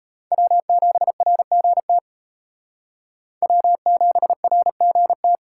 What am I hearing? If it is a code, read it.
Morse code, heard as a keyed tone, about 33 wpm, 700 Hz. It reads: W7RGT W7RGT
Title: Speed Racing - USA 1x3 Call Signs 28wpm